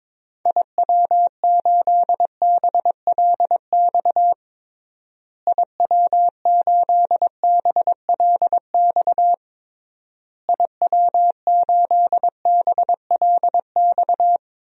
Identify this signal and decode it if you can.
Morse code, heard as a keyed tone, about 21 wpm, 700 Hz. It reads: IW8BLX IW8BLX IW8BLX
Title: DXCC-100 Call Signs - 6 Characters - Repeated 3x - 22wpm